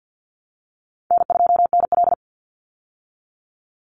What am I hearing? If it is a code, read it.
Morse code, heard as a keyed tone, about 50 wpm, 700 Hz. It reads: N2NL
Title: Call Signs Repeated 2x - 50wpm